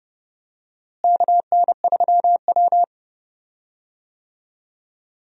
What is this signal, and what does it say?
Morse code, heard as a keyed tone, about 30 wpm, 700 Hz. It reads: KN3W
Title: USA Call Signs - 4 Characters - Repeated 2x - 30wpm